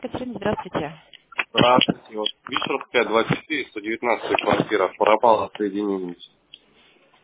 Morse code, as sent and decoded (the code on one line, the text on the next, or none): none